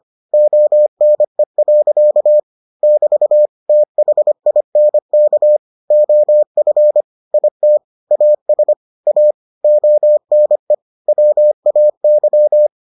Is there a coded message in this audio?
Yes: Morse code from 0.3 to 12.7 s, heard as a keyed tone, about 25 wpm, 600 Hz. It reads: ONE. <BT> THINK OF IT AS A ONE WAY